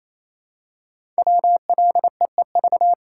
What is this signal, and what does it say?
Morse code, heard as a keyed tone, about 27 wpm, 690 Hz. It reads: WLEEV
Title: Code Groups - 5 Letters - 28wpm